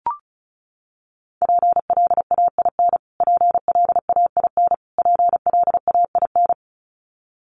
Morse code, heard as a keyed tone, about 34 wpm, 700 Hz. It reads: PLAIN PLAIN PLAIN
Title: Top 500 Words Repeated 35wpm